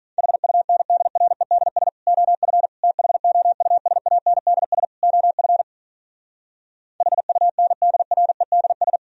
Morse code, heard as a keyed tone, about 46 wpm, 700 Hz. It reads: HUNDREDS OF THOUSANDS OF HUNDREDS